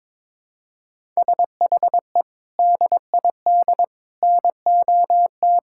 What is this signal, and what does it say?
Morse code, heard as a keyed tone, about 22 wpm, 700 Hz. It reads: SHE DID NOT
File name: practice / 22wpm / Build A Phrase - 5-Word Phrases - 22wpm